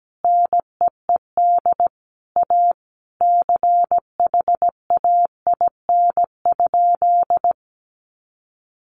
Morse code, heard as a keyed tone, about 17 wpm, 700 Hz. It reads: NEED A CHAIN?
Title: Sentences from Top 1500 Words 17wpm